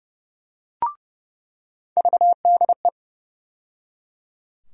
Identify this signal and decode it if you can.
Morse code, heard as a keyed tone, about 30 wpm, 700 Hz. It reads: VDE
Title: USA Call Sign Postfixes - 3 Letters 30wpm